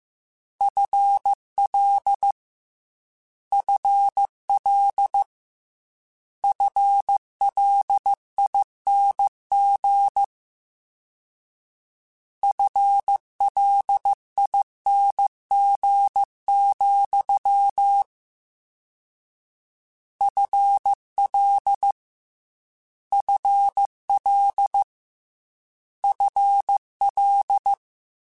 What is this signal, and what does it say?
Morse code, heard as a keyed tone, about 15 wpm, 800 Hz. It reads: FL FL FLING FLING, FL FL FL
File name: Combinations / Combinations-mp3s / Combinations@13